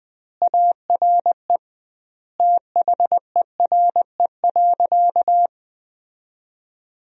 Morse code, heard as a keyed tone, about 20 wpm, 700 Hz. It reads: ARE THERE.
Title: 2x Word-Spacing - Sentences from Top 500 Words - 20wpm